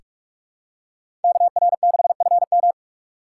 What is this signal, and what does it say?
Morse code, heard as a keyed tone, about 46 wpm, 700 Hz. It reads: KR6FM
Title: USA Call Signs - 5 Characters - 2x Character Spacing - 45wpm